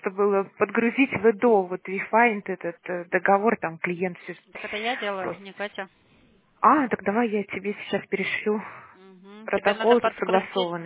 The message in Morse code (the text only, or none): none